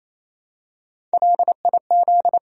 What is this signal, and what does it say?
Morse code, heard as a keyed tone, about 28 wpm, 700 Hz. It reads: LIZ
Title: USA Common Names - Max 3 Letters - 28wpm